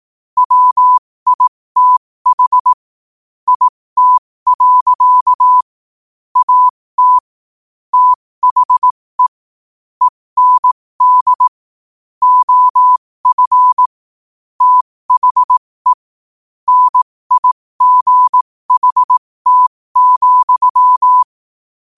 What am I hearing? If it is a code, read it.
Morse code, heard as a keyed tone, about 18 wpm, spaced as for 15 wpm, 1000 Hz. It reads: WITH IT. AT THE END OF THE NIGHT,
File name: 15wpm